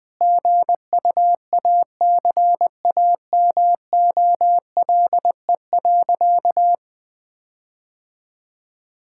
Morse code, heard as a keyed tone, about 19 wpm, 700 Hz. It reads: GUACAMOLE.
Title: All-Age Appropriate Jokes - 20wpm